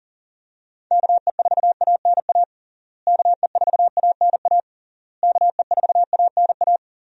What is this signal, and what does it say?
Morse code, heard as a keyed tone, about 40 wpm, 700 Hz. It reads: KE4ANA KE4ANA KE4ANA
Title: USA Call Signs - 6 Characters - Repeated 3x - 40wpm